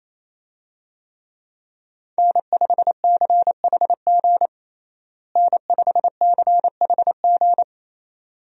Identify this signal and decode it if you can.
Morse code, heard as a keyed tone, about 28 wpm, 700 Hz. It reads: N5CHG N5CHG
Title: USA Call Signs - 5 Characters - Repeated 2x - 28wpm